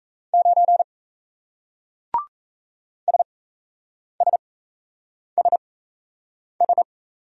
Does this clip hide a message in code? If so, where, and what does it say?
Yes: Morse code from 0.3 to 6.8 s, heard as a keyed tone, about 36 wpm, spaced as for 8 wpm, 700 Hz. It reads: 9 SSSS